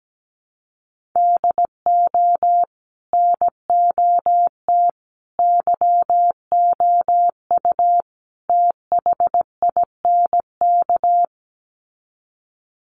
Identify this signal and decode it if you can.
Morse code, heard as a keyed tone, about 17 wpm, 700 Hz. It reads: DO NOT YOU THINK
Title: Sets of 4 Words 17wpm